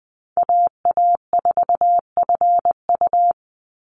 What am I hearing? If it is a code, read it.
Morse code, heard as a keyed tone, about 20 wpm, 700 Hz. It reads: AA4FU
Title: Call Signs Repeated 2x - 20wpm